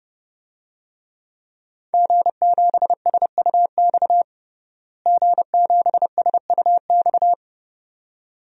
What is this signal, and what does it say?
Morse code, heard as a keyed tone, about 30 wpm, 700 Hz. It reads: G7SUX G7SUX
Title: DXCC-100 Call Signs - 5 Characters - Repeated 2x - 30wpm